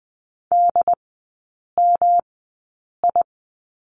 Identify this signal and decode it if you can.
Morse code, heard as a keyed tone, about 20 wpm, spaced as for 8 wpm, 700 Hz. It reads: DMI
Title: 152 Lesson 25e - Letter p - Three Characters 20wpm